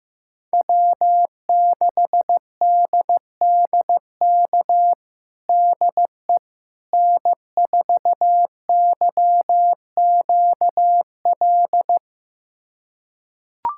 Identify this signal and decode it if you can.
Morse code, heard as a keyed tone, about 15 wpm, 700 Hz. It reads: W6DDK DE N4YQL
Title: USA Callsign DE Callsign - 5 Characters - 15wpm